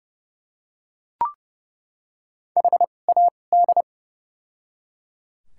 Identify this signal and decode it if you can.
Morse code, heard as a keyed tone, about 30 wpm, spaced as for 22 wpm, 700 Hz. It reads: HAD
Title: 2x Character Spacing - Top 100 Words - 30wpm